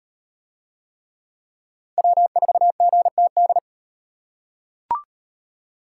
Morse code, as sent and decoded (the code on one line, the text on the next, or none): .-- ....- --. - -..
W4GTD